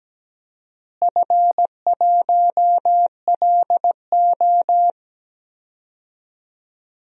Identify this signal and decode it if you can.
Morse code, heard as a keyed tone, about 17 wpm, 700 Hz. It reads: F1LO
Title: DXCC-100 Call Signs - 4 Characters - Repeated 2x - 17wpm